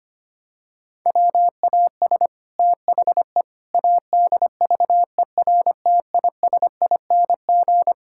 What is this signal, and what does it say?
Morse code, heard as a keyed tone, about 25 wpm, 700 Hz. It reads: WAS THE ADVERTISING
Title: Sentences from Top 1000 Words - Review Part 6 - 25wpm